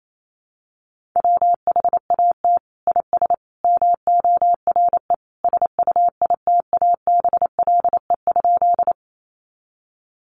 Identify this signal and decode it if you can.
Morse code, heard as a keyed tone, about 28 wpm, 700 Hz. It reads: WHAT IS MORE SUITABLE?
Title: Sentences from Top 1400 Words 28wpm